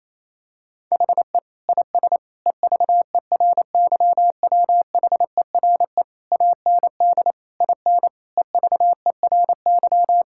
Morse code, heard as a keyed tone, about 28 wpm, 700 Hz. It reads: HE IS EVERYWHERE AND IN EVERY
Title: Sentences from Top 1600 Words 28wpm